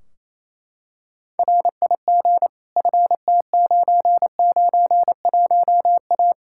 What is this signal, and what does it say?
Morse code, heard as a keyed tone, about 28 wpm, 700 Hz. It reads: RIG FT991A
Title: Practice QSO - Exchange 2 - Info - 28wpm